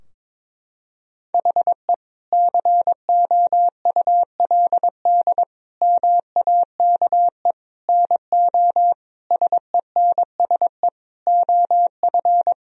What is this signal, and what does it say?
Morse code, heard as a keyed tone, about 22 wpm, 700 Hz. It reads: HE COULD MAKE NO SENSE OF
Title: Sentences from Top 1000 Words - Review Part 4 - 22wpm